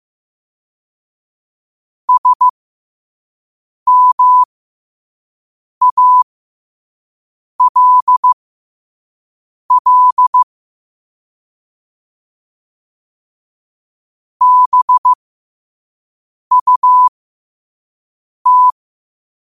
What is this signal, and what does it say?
Morse code, heard as a keyed tone, about 15 wpm, spaced as for 5 wpm, 1000 Hz. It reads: SMALL BUT
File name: 5wpm